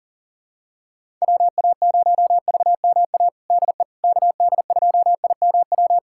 Morse code, heard as a keyed tone, about 37 wpm, 700 Hz. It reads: WA0VMA DE KD2IMW
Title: USA Callsign DE Callsign 40wpm